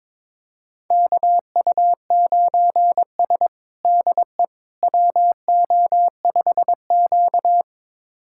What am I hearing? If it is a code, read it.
Morse code, heard as a keyed tone, about 22 wpm, 700 Hz. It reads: KU9S DE WO5Q
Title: USA Callsign DE Callsign - 4 Characters - 22wpm